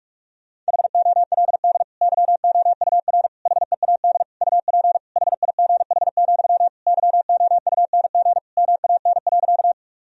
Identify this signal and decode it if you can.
Morse code, heard as a keyed tone, about 46 wpm, 700 Hz. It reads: HOLD YOUR HEAD UP HIGH, YOUNG MAN.
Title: Sentences from Top 1000 Words - Review Part 2 - 45wpm